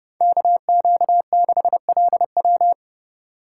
Morse code, heard as a keyed tone, about 37 wpm, 700 Hz. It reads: KQ6LW
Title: Speed Racing - USA Call Signs 30wpm